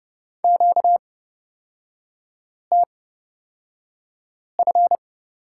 Morse code, heard as a keyed tone, about 31 wpm, spaced as for 5 wpm, 700 Hz. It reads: QTF
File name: Lesson 38e - Letter q - Three Characters 30wpm